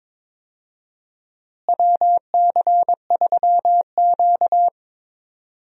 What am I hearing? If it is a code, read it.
Morse code, heard as a keyed tone, about 22 wpm, 700 Hz. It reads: WC3Q
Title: USA Call Signs - 4 Characters - Repeated 2x - 22wpm